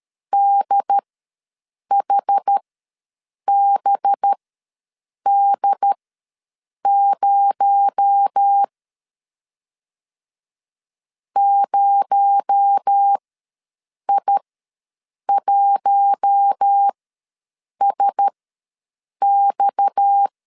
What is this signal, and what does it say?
Morse code, heard as a keyed tone, about 13 wpm, spaced as for 7 wpm, 800 Hz. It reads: DHBD0 0I1SX
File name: LEZIONI / 6 PKX0 / 6 PKX0 ws 12wpm 3448